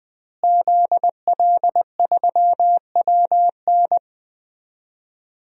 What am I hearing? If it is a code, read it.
Morse code, heard as a keyed tone, about 20 wpm, 700 Hz. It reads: ZL3WN
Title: DXCC-100 Call Signs - 5 Characters - Repeated 3x - 20wpm